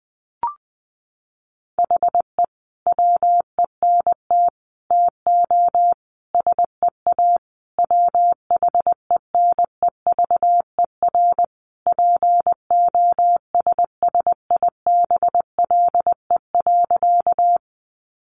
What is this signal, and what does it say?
Morse code, heard as a keyed tone, about 20 wpm, 700 Hz. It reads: HE WENT TO SEA WHENEVER POSSIBLE.
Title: Sentences from Top 1200 Words 20wpm